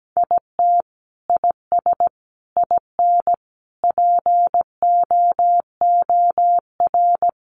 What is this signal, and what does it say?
Morse code, heard as a keyed tone, about 17 wpm, 700 Hz. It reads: IT IS IN POOR
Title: Sentences from Top 1200 Words 17wpm